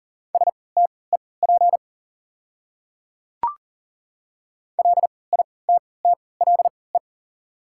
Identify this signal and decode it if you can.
Morse code, heard as a keyed tone, about 40 wpm, spaced as for 23 wpm, 690 Hz. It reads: STEP LITTLE
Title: 3x Character Spacing - Top 500 Words - 40wpm